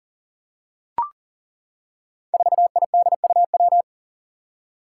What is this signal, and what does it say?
Morse code, heard as a keyed tone, about 39 wpm, 700 Hz. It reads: 4IDUW
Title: Five Challenging Letter-Number Combinations 40wpm